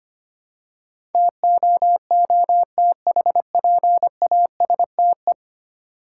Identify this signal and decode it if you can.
Morse code, heard as a keyed tone, about 25 wpm, 700 Hz. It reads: TOOTHPASTE
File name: practice / 25wpm / Easy Words with Hints - 25wpm